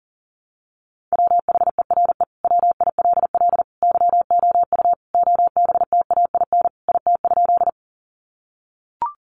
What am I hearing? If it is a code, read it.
Morse code, heard as a keyed tone, about 40 wpm, 700 Hz. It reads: WHERE WILL YOU OBTAIN IT?
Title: Sentences from Top 1300 Words 40wpm